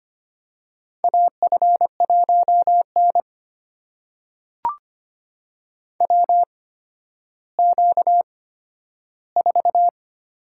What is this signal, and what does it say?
Morse code, heard as a keyed tone, about 25 wpm, spaced as for 12 wpm, 700 Hz. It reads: AF1N WQ4
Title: USA Call Signs - 4 Characters - 8x Character Spacing - 25wpm